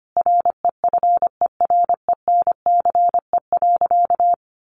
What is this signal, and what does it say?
Morse code, heard as a keyed tone, about 25 wpm, 700 Hz. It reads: REFERENCE.
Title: Sentences from Top 1200 Words 25wpm